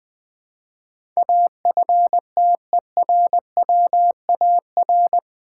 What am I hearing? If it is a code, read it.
Morse code, heard as a keyed tone, about 20 wpm, 700 Hz. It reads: AFTERWAR
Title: Build A Word - 4 Letters At A Time - 10-Letter Words - 20wpm